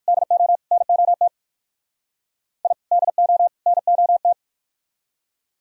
Morse code, heard as a keyed tone, about 56 wpm, 700 Hz. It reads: DO NOT I DO NOT
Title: Speed-Racing - Sets of 3 Words 45wpm